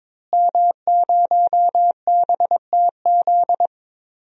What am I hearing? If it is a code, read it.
Morse code, heard as a keyed tone, about 22 wpm, 700 Hz. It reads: M0BTZ
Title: DXCC-100 Call Signs - 5 Characters - Repeated 3x - 22wpm